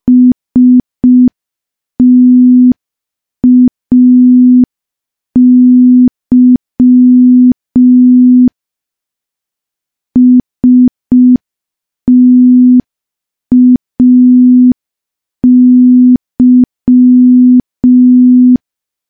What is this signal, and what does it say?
Morse code, heard as a keyed tone, about 5 wpm, 250 Hz. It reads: STAY STAY
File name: morse / morse-training-05WPM